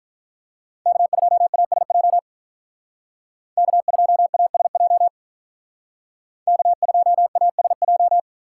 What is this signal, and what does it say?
Morse code, heard as a keyed tone, about 43 wpm, 700 Hz. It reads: K2ASJ K2ASJ K2ASJ
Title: Speed Racing - USA 1x3 Call Signs 35wpm